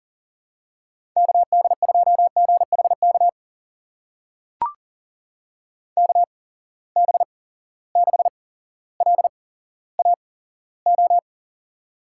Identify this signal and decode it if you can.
Morse code, heard as a keyed tone, about 41 wpm, spaced as for 17 wpm, 700 Hz. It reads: KD2GHK KB6LAO
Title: USA Call Signs - 6 Characters - 8x Character Spacing - 40wpm